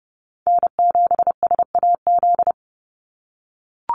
Morse code, heard as a keyed tone, about 30 wpm, 700 Hz. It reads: N7SAZ
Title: USA Call Signs 30wpm